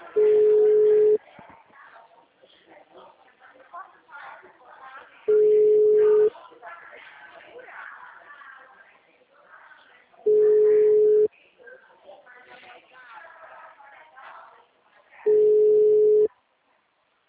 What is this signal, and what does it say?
Ringback tone (Europe, 425 Hz), with speech.